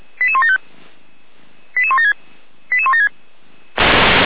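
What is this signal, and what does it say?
Five-tone selective call (ZVEI), heard as quick runs of five tones, tones of about 70 ms. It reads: 88175 88176 88176
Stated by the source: ZVEI five-tone selective call received by SDR.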